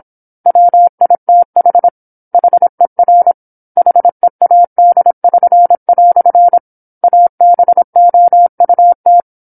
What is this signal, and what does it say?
Morse code, heard as a keyed tone, about 27 wpm, 700 Hz. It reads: WITH HER HEAD<SN>" ABOUT
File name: 08-Chapter-08 26 WPM